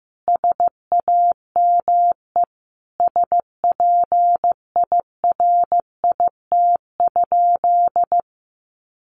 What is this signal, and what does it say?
Morse code, heard as a keyed tone, about 15 wpm, 700 Hz. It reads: SAME SPIRIT?
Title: Sentences from Top 1400 Words 15wpm